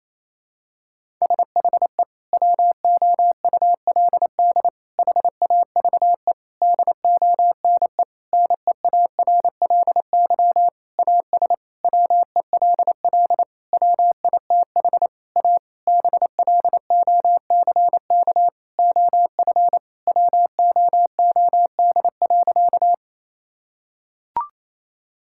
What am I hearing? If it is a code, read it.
Morse code, heard as a keyed tone, about 28 wpm, 700 Hz. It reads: SHE WOULD HAVE DONE NEARLY AS WELL WITH A BLOCK OF WOOD.
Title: Sentences from Top 1600 Words 28wpm